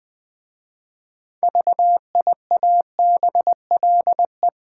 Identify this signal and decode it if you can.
Morse code, heard as a keyed tone, about 20 wpm, 700 Hz. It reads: VIABLE
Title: 6 Letter Words - First 3 Letters Given - 20wpm